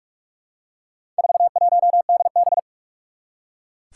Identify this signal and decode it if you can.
Morse code, heard as a keyed tone, about 44 wpm, 700 Hz. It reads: 41DB
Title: Four Challenging Letter-Number Combinations 45wpm